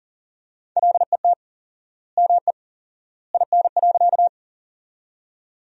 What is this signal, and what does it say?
Morse code, heard as a keyed tone, about 40 wpm, 700 Hz. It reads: LET ME IN.
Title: 4x Word-Spacing - Sentences from Top 300 Words - 40wpm